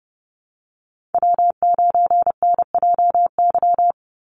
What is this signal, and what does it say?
Morse code, heard as a keyed tone, about 30 wpm, 700 Hz. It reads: W9NJY
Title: Call Signs Repeated 3x - 30wpm